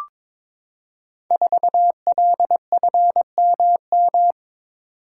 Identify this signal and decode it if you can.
Morse code, heard as a keyed tone, about 22 wpm, 700 Hz. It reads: 4LFMM
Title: Code Groups - 5 Letter:Numbers - 22wpm